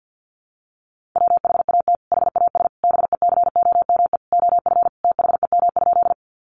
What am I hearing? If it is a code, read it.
Morse code, heard as a keyed tone, about 50 wpm, 700 Hz. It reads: WHAT HAS BECOME OF THEM?
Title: Sentences from Top 600 Words 50wpm